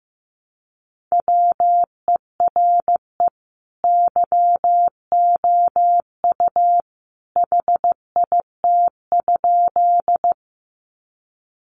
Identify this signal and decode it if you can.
Morse code, heard as a keyed tone, about 15 wpm, 700 Hz. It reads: WERE YOU HIT?
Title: Sentences from Top 1200 Words 15wpm